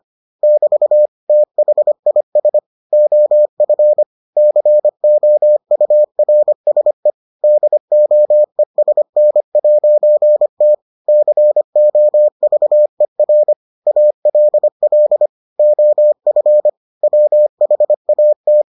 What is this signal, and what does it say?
Morse code, heard as a keyed tone, about 25 wpm, 600 Hz. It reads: <BT> THIS OF COURSE DOESN'T COVER ALL OF WHAT